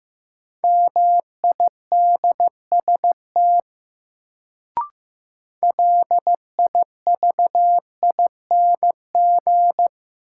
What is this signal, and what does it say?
Morse code, heard as a keyed tone, about 15 wpm, 700 Hz. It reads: MIDST LIVING
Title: Sets of 4 Words - New Word Drill - 15wpm